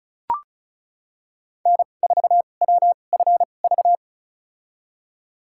Morse code, heard as a keyed tone, about 36 wpm, spaced as for 25 wpm, 700 Hz. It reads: N4WFV
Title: USA Call Signs - 5 Characters - 2x Character Spacing - 35wpm